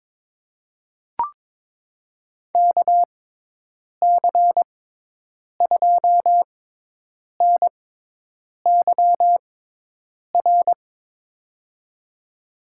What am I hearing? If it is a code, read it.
Morse code, heard as a keyed tone, about 22 wpm, spaced as for 8 wpm, 700 Hz. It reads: KC2NYR
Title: USA Call Signs - 6 Characters - 6x Character Spacing - 22wpm